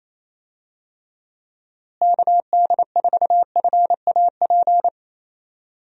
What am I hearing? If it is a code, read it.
Morse code, heard as a keyed tone, about 28 wpm, 700 Hz. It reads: KD4FAP
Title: USA Call Signs - 6 Characters - Repeated 2x - 28wpm